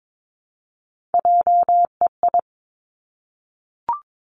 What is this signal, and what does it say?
Morse code, heard as a keyed tone, about 22 wpm, 700 Hz. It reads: JEI
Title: Three Challenging Letter-Number Combinations 22wpm